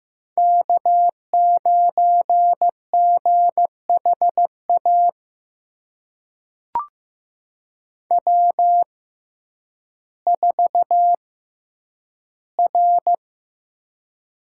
Morse code, heard as a keyed tone, about 15 wpm, spaced as for 9 wpm, 700 Hz. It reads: K9GHA W4R
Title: USA Call Signs - 5 Characters - 6x Character Spacing - 15wpm